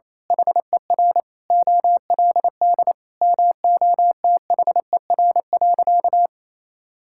Morse code, heard as a keyed tone, about 28 wpm, 700 Hz. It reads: HER OLD MOTHER.
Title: Sentences from Top 1000 Words - Review Part 5 - 28wpm